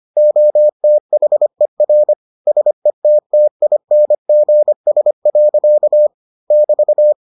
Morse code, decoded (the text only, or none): OTHER SETTINGS. <BT>